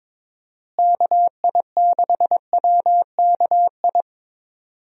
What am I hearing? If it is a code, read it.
Morse code, heard as a keyed tone, about 22 wpm, 700 Hz. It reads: KI6WKI
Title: USA Call Signs - 6 Characters - Repeated 3x - 22wpm